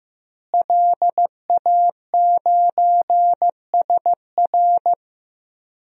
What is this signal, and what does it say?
Morse code, heard as a keyed tone, about 15 wpm, 700 Hz. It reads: LA9SR
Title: DXCC-100 Call Signs - 5 Characters - Repeated 3x - 15wpm